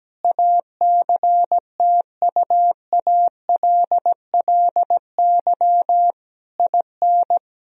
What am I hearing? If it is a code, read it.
Morse code, heard as a keyed tone, about 17 wpm, 700 Hz. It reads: ACTUALLY IN
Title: Sentences from Top 1000 Words - Review Part 4 - 17wpm